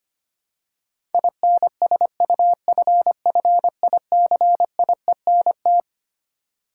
Morse code, heard as a keyed tone, about 25 wpm, 700 Hz. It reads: INSUFFICIENT
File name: Twelve Letter Words - 25wpm